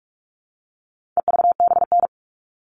Morse code, heard as a keyed tone, about 44 wpm, 700 Hz. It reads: EVBN
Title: All Cut Numbers - 4-Digit 45wpm